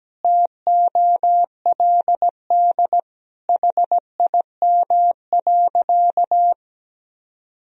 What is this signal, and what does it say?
Morse code, heard as a keyed tone, about 17 wpm, 700 Hz. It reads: TOLD HIM.
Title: Sentences from Top 1000 Words - Review Part 1 - 17wpm